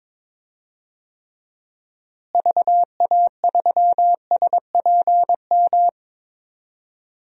Morse code, heard as a keyed tone, about 22 wpm, 700 Hz. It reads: VA3SPM
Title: DXCC-100 Call Signs - 6 Characters - Repeated 3x - 22wpm